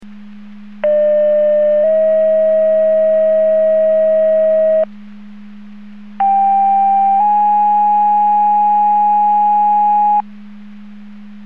Two-tone page, heard as a one-second tone, then a three-second tone, about 600 then 640 Hz, about 790 then 830 Hz. It is heard from 0.8 to 10.2 s.
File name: Portage 154.385 Paging